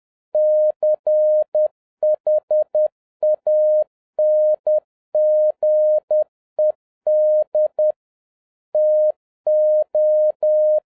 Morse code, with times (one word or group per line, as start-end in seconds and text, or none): 0.3-7.9 CHANGED
8.7-10.8 TO